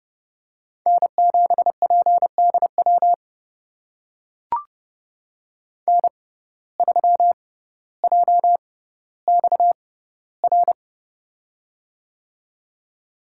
Morse code, heard as a keyed tone, about 30 wpm, spaced as for 15 wpm, 700 Hz. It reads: N7PDW N3JXR
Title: USA Call Signs - 5 Characters - 6x Character Spacing - 30wpm